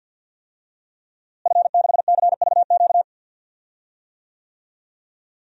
Morse code, heard as a keyed tone, about 46 wpm, 700 Hz. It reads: F6CVQ
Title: DXCC-100 Call Signs - 5 Characters - Repeated 3x - 50wpm